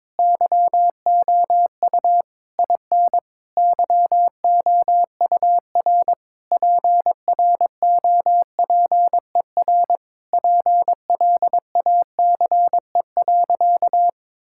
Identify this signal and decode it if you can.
Morse code, heard as a keyed tone, about 22 wpm, 700 Hz. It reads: YOU IN YOUR PROPER PLACE.
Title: Sentences from Top 1000 Words - Review Part 5 - 22wpm